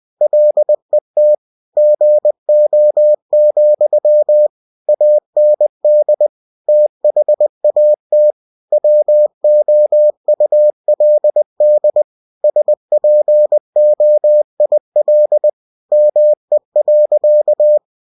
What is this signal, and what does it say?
Morse code, heard as a keyed tone, about 20 wpm, 600 Hz. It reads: LET GO, AND THAT WOULD SPOIL ME.